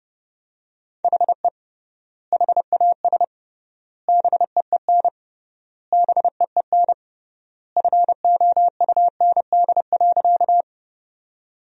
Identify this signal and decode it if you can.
Morse code, heard as a keyed tone, about 30 wpm, 700 Hz. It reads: HE HAS BEEN BEEN FOUND.
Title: 3x Word-Spacing - Sentences from Top 300 Words - 30wpm